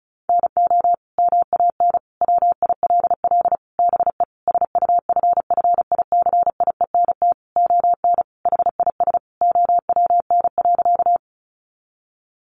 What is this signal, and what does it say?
Morse code, heard as a keyed tone, about 34 wpm, 700 Hz. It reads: NO MAN WILL BE SUFFICIENT ON HIS OWN.
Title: Sentences from Top 1400 Words 35wpm